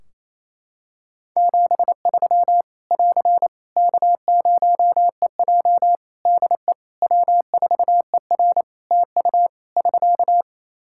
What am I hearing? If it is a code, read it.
Morse code, heard as a keyed tone, about 28 wpm, 700 Hz. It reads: 73 <AR> K0EJ DE W4ER TU <SK>
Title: Practice QSO - Exchange 4 - Ending - 28wpm